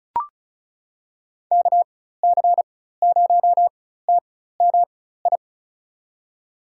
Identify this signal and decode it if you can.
Morse code, heard as a keyed tone, about 36 wpm, spaced as for 16 wpm, 700 Hz. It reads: KC0TMI
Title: USA Call Signs - 6 Characters - 4x Character Spacing - 35wpm